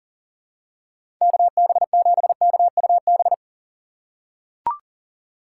Morse code, heard as a keyed tone, about 40 wpm, 700 Hz. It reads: KB7KUB
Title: USA Call Signs - 6 Characters - 6x Character Spacing - 40wpm